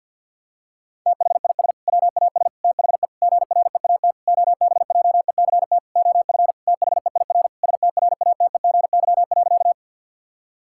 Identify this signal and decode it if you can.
Morse code, heard as a keyed tone, about 52 wpm, 700 Hz. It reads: THIS WAS THE GREAT OBJECT OF THEIR STRATEGY.
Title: Sentences from Top 1000 Words - Review Part 5 - 50wpm